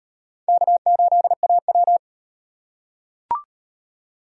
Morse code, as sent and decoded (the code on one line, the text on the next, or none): -.- ---.. .- .--
K8AW